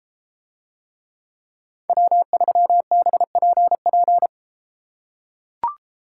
Morse code, heard as a keyed tone, about 33 wpm, 700 Hz. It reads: W3BPP